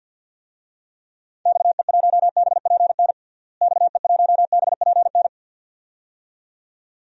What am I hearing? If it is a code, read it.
Morse code, heard as a keyed tone, about 51 wpm, 700 Hz. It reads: XE1BPN XE1BPN
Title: DXCC-100 Call Signs - 6 Characters - Repeated 2x - 50wpm